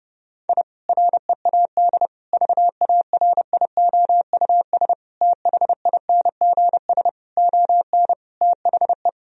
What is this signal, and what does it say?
Morse code, heard as a keyed tone, about 30 wpm, 700 Hz. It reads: I READ VARIOUS THINGS ON THE
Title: Sentences from Top 1000 Words - Review Part 4 - 30wpm